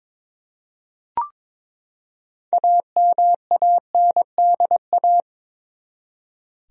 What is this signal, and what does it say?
Morse code, heard as a keyed tone, about 22 wpm, 700 Hz. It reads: AMANDA
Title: USA Common Names - 22wpm